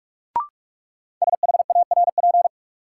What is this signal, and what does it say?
Morse code, heard as a keyed tone, about 42 wpm, 700 Hz. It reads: SHARP
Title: Sentences from Top 1600 Words - New Word Drill - 45wpm